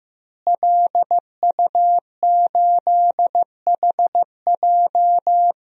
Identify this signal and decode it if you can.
Morse code, heard as a keyed tone, about 15 wpm, 700 Hz. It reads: LU8HJ
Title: DXCC-100 Call Signs - 5 Characters - Repeated 3x - 15wpm